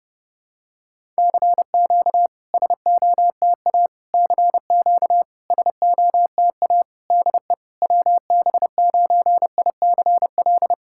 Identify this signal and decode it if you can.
Morse code, heard as a keyed tone, about 30 wpm, 660 Hz. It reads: CQ SOTA CQ SOTA DE WB9ICL
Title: SOTA QSO Practice 30wpm